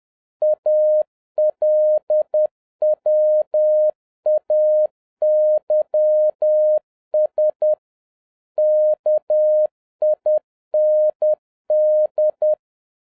Morse code, heard as a keyed tone, about 10 wpm, 610 Hz. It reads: ALWAYS KIND